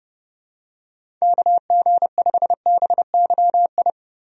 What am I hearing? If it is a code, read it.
Morse code, heard as a keyed tone, about 30 wpm, 700 Hz. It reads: KG5BYI
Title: USA Call Signs - 6 Characters - Repeated 3x - 30wpm